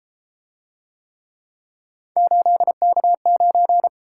Morse code, heard as a keyed tone, about 33 wpm, 700 Hz. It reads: 8K9